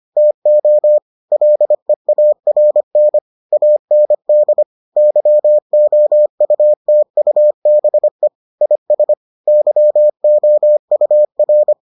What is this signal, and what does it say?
Morse code, heard as a keyed tone, about 25 wpm, 600 Hz. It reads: TO LEARN AND YOUTUBE IS YOUR